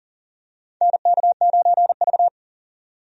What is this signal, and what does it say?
Morse code, heard as a keyed tone, about 40 wpm, 700 Hz. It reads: NK9V
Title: USA Call Signs - 4 Characters - 40wpm